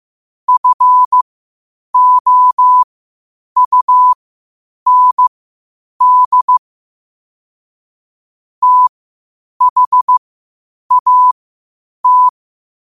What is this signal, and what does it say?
Morse code, heard as a keyed tone, about 15 wpm, spaced as for 8 wpm, 1000 Hz. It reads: FOUND THAT
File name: qotdfiles / 8wpm